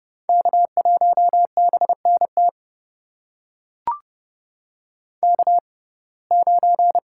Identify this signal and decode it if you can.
Morse code, heard as a keyed tone, about 30 wpm, spaced as for 22 wpm, 700 Hz. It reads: K1BNT K9
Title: USA Call Signs - 5 Characters - 6x Character Spacing - 30wpm